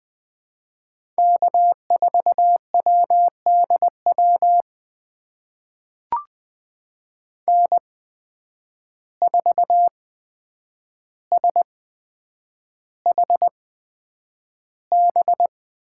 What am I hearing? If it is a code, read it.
Morse code, heard as a keyed tone, about 19 wpm, spaced as for 8 wpm, 700 Hz. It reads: K4WDW N4SHB
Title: USA Call Signs - 5 Characters - 8x Character Spacing - 20wpm